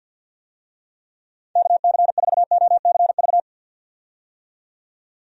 Morse code, heard as a keyed tone, about 51 wpm, 700 Hz. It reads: KC4OCV
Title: USA Call Signs - 6 Characters - 50wpm